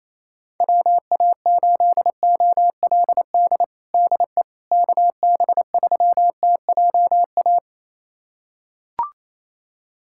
Morse code, heard as a keyed tone, about 28 wpm, 700 Hz. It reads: WA8OLD DE KB3TJA
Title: USA Callsign DE Callsign - 6 Characters - 28wpm